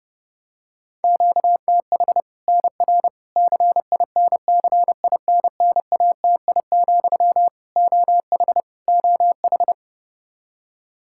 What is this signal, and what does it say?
Morse code, heard as a keyed tone, about 30 wpm, 700 Hz. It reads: QTH NR CINCINNATI, OH OH
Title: Practice QSO - Exchange 1 - Beginning - 30wpm